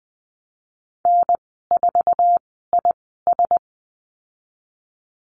Morse code, heard as a keyed tone, about 20 wpm, spaced as for 15 wpm, 700 Hz. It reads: N4IS
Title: 076 Lesson 13g - Letter L - Call Signs 20wpm